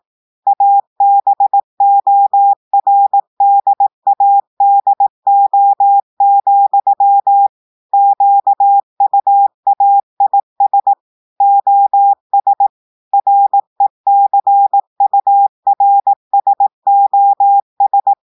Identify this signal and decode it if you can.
Morse code, heard as a keyed tone, about 18 wpm, 800 Hz. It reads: ABORDADO, QUAIS OS RECURSOS
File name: portugais texte en clair 2 vitesse 18 mots minutes